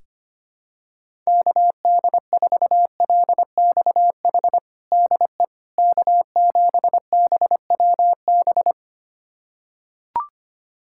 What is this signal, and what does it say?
Morse code, heard as a keyed tone, about 25 wpm, 700 Hz. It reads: KD4LXH DE K7BWB
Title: USA Callsign DE Callsign 25wpm